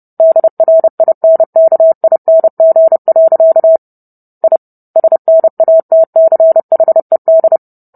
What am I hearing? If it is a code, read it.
Morse code, heard as a keyed tone, about 30 wpm, 650 Hz. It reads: DRINKING. I SNATCHED